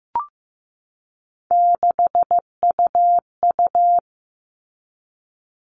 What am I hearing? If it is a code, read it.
Morse code, heard as a keyed tone, about 15 wpm, 700 Hz. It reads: 6UU
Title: All Cut Numbers - 3-Digit 15wpm